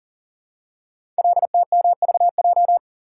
Morse code, heard as a keyed tone, about 40 wpm, 700 Hz. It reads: LTMVJ
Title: Code Groups - 5 Letters - 40wpm